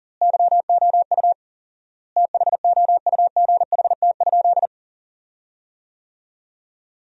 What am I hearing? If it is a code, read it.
Morse code, heard as a keyed tone, about 40 wpm, 700 Hz. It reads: YOU THOUGHT?
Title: 4x Word-Spacing - Sentences from Top 300 Words - 40wpm